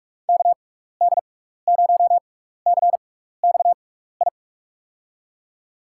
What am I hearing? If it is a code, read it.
Morse code, heard as a keyed tone, about 43 wpm, spaced as for 15 wpm, 700 Hz. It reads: KD0CXI
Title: USA Call Signs - 6x Character Spacing - 45wpm